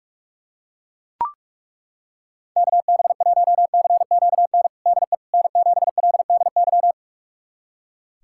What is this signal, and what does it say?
Morse code, heard as a keyed tone, about 41 wpm, 700 Hz. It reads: KB1CQN DE N7LDY
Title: USA Callsign DE Callsign 45wpm